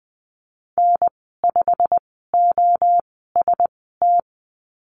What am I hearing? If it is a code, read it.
Morse code, heard as a keyed tone, about 20 wpm, spaced as for 15 wpm, 700 Hz. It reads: N5OST
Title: 088 Lesson 15g - Number 5 - Call Signs 20wpm